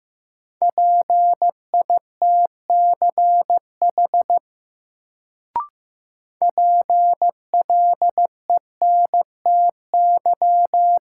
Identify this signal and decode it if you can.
Morse code, heard as a keyed tone, about 15 wpm, 700 Hz. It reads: PITCH PLENTY
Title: Sentences from Top 1300 Words - New Word Drill Alt - 15wpm